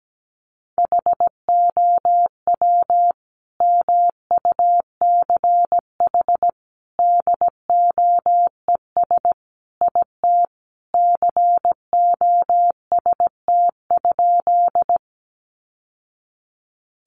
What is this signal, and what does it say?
Morse code, heard as a keyed tone, about 17 wpm, 700 Hz. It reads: HOW MUCH DOES IT COST?
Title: Sentences from Top 800 Words 17wpm